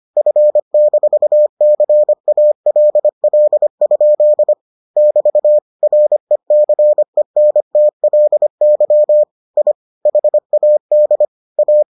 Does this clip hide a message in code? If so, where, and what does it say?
Yes: Morse code from 0.2 to 11.8 s, heard as a keyed tone, about 25 wpm, 600 Hz. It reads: F-CALL? <BT> RECENTLY I HAD A